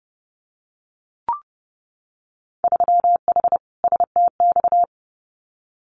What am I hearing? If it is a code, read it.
Morse code, heard as a keyed tone, about 30 wpm, 700 Hz. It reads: 3H STX
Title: Winter Field Day Exchanges 30wpm